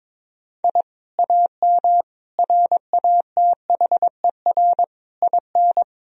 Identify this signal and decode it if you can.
Morse code, heard as a keyed tone, about 22 wpm, 700 Hz. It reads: I AM RATHER IN
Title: Sentences from Top 1000 Words - Review Part 4 - 22wpm